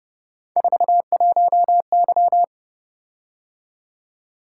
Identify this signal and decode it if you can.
Morse code, heard as a keyed tone, about 29 wpm, 700 Hz. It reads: 41Y